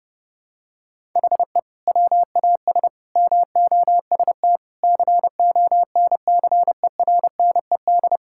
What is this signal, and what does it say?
Morse code, heard as a keyed tone, about 31 wpm, 700 Hz. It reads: HE WAS MOST CONCERNED
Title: Sentences from Top 1000 Words - Review Part 6 - 30wpm